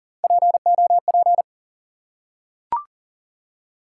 Morse code, heard as a keyed tone, about 38 wpm, 700 Hz. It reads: POP
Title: Three Letter Words - 40wpm